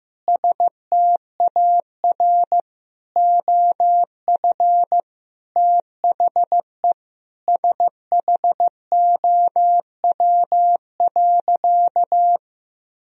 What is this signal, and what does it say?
Morse code, heard as a keyed tone, about 15 wpm, 700 Hz. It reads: STAR OF THE SHOW.
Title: Sentences from Top 1000 Words - Review Part 2 - 15wpm